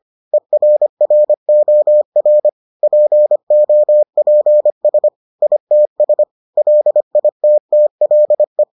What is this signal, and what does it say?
Morse code, heard as a keyed tone, about 25 wpm, 600 Hz. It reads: ERROR POPS ITS LITTLE